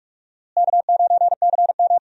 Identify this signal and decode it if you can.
Morse code, heard as a keyed tone, about 41 wpm, 700 Hz. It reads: K9CM
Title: USA Call Signs - 4 Characters - Repeated 2x - 45wpm